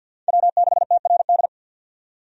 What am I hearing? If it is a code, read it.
Morse code, heard as a keyed tone, about 50 wpm, 700 Hz. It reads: W6TRD
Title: USA Call Signs - 5 Characters - 50wpm